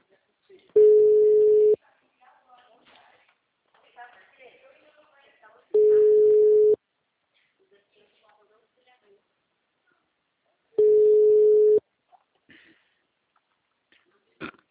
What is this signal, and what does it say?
Ringback tone (Europe, 425 Hz), with speech.